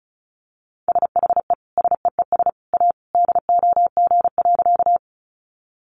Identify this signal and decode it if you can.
Morse code, heard as a keyed tone, about 34 wpm, 700 Hz. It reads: SHE SEES A DOG.
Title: Easy Phrases 35wpm